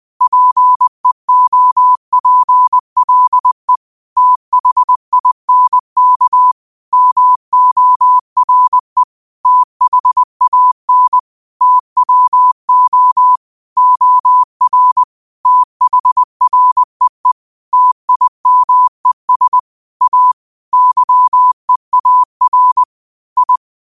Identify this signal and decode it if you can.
Morse code, heard as a keyed tone, about 20 wpm, 1000 Hz. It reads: PEOPLE THINK MORE THAN TWO OR THREE TIMES A YEAR I
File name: qotdfiles / 20wpm